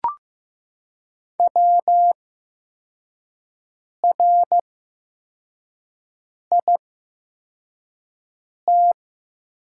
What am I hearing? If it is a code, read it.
Morse code, heard as a keyed tone, about 15 wpm, spaced as for 4 wpm, 700 Hz. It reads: WRIT
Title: 8x Character Spacing - Top 200 Words - 15wpm